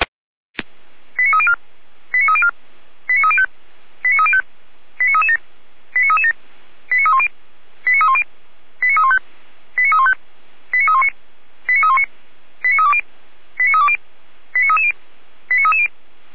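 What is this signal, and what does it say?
Five-tone selective call (ZVEI), heard as quick runs of five tones, tones of about 70 ms. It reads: 89304 89304 89305 89305 89337 89337 89310 89310 89315 89315 89319 89319 89320 89320 89330 89330
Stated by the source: ZVEI five-tone call received by SDR.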